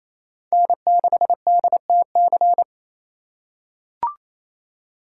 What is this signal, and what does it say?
Morse code, heard as a keyed tone, about 27 wpm, 700 Hz. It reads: N6DTC
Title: USA Call Signs - 5 Characters - 2x Character Spacing - 28wpm